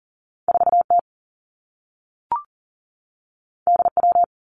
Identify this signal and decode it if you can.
Morse code, heard as a keyed tone, about 39 wpm, 700 Hz. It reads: 4 T D W
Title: Two Challenging Letter-Number Combinations 40wpm